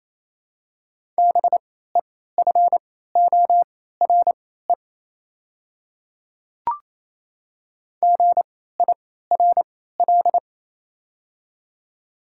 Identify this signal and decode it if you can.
Morse code, heard as a keyed tone, about 28 wpm, spaced as for 16 wpm, 700 Hz. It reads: BEFORE GIRL
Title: 3x Character Spacing - Top 500 Words - 28wpm